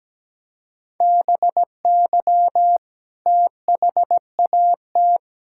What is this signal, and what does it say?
Morse code, heard as a keyed tone, about 17 wpm, 700 Hz. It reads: BY THAT
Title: Sentences from Top 1600 Words 17wpm